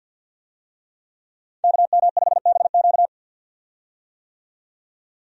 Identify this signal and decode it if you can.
Morse code, heard as a keyed tone, about 51 wpm, 700 Hz. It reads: KM5BQ
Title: USA Call Signs - 5 Characters - 50wpm